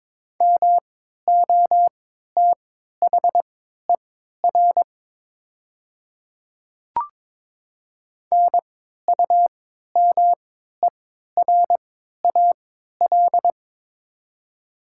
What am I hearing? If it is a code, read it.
Morse code, heard as a keyed tone, about 22 wpm, spaced as for 13 wpm, 700 Hz. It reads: MOTHER NUMERAL
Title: 3x Character Spacing - Top 500 Words - 22wpm